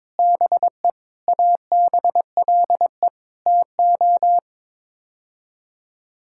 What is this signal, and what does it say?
Morse code, heard as a keyed tone, about 21 wpm, 700 Hz. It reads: BE ABLE TO
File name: practice / 22wpm / Build A Phrase - 5-Word Phrases - 22wpm